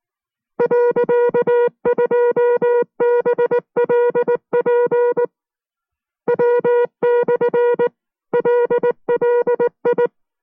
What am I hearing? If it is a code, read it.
Morse code, heard as a keyed tone, about 19 wpm, 480 Hz. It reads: .2BLP W/LLI